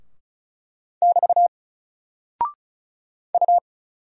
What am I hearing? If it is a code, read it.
Morse code, heard as a keyed tone, about 36 wpm, 700 Hz. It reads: <BT> U
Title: Single Character (Letter:Number:Punctuation:Prosign) Alt 35wpm